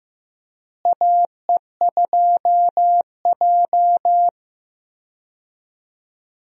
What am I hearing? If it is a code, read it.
Morse code, heard as a keyed tone, about 15 wpm, 700 Hz. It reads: AE2J
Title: USA Call Signs - 4 Characters - Repeated 3x - 15wpm